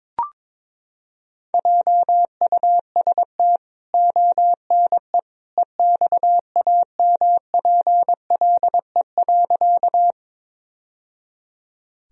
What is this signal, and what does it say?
Morse code, heard as a keyed tone, about 22 wpm, 700 Hz. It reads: JUST ONE EXAMPLE.
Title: Sentences from Top 1000 Words - Review Part 1 - 22wpm